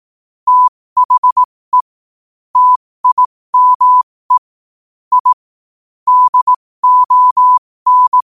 Morse code, heard as a keyed tone, about 18 wpm, spaced as for 15 wpm, 1000 Hz. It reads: THE TIME I DON